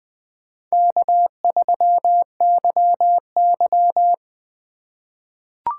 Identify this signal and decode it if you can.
Morse code, heard as a keyed tone, about 20 wpm, 700 Hz. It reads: K3YY
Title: USA Call Signs - 4 Characters - 2x Character Spacing - 20wpm